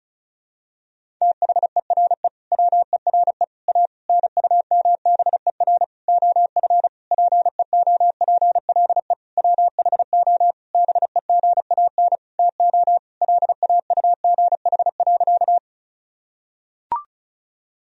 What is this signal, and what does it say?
Morse code, heard as a keyed tone, about 36 wpm, 700 Hz. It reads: THERE WERE A NUMBER OF PEOPLE WHO BEGAN TO LAUGH.
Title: Sentences from Top 1000 Words - Review Part 1 - 35wpm